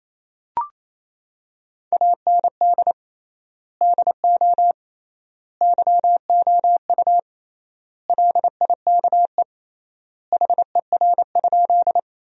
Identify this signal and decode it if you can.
Morse code, heard as a keyed tone, about 27 wpm, 700 Hz. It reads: AND DO YOU LIKE HER?
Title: 3x Word-Spacing - Sentences from Top 100 Words - 28wpm